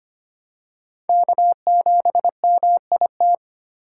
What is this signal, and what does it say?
Morse code, heard as a keyed tone, about 25 wpm, 700 Hz. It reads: K7MIT
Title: USA Call Signs - 5 Characters - 4x Character Spacing - 25wpm